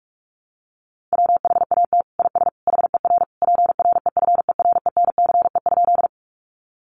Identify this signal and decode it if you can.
Morse code, heard as a keyed tone, about 44 wpm, 700 Hz. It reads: WHAT IS HER PREFERENCE?
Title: Sentences from Top 1500 Words 45wpm